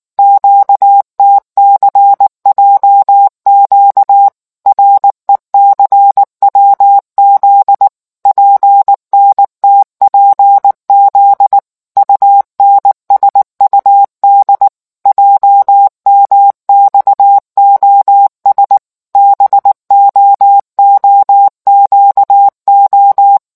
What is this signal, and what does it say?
Morse code, heard as a keyed tone, about 19 wpm, 800 Hz. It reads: QTCJQ RECWZ PNTPZ UNSUD JMXOS BOOQO